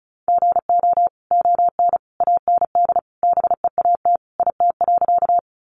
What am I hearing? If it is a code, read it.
Morse code, heard as a keyed tone, about 34 wpm, 700 Hz. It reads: GO ON AND BEAT IT.
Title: Sentences from Top 1300 Words 35wpm